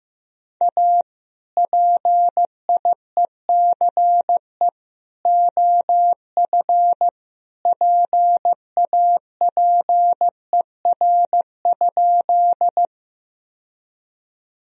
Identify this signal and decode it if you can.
Morse code, heard as a keyed tone, about 15 wpm, 700 Hz. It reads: A PIECE OF PAPER?
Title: Sentences from Top 1000 Words - Review Part 2 - 15wpm